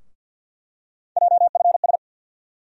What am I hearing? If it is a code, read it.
Morse code, heard as a keyed tone, about 52 wpm, 700 Hz. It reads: JFS
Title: USA Call Sign Postfixes - 3 Letters 50wpm